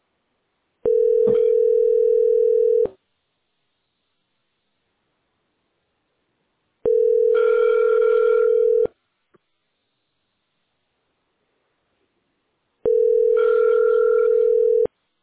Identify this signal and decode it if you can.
Ringback tone (North America), 440 + 480 Hz.